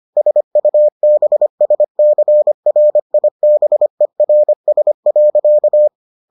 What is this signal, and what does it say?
Morse code, heard as a keyed tone, about 25 wpm, 600 Hz. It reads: SUBSCRIBERS.